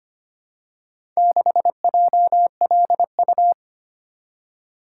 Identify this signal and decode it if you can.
Morse code, heard as a keyed tone, about 25 wpm, 690 Hz. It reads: 6JLU
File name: practice / 25wpm / Four Letter-Number Combinations - 25wpm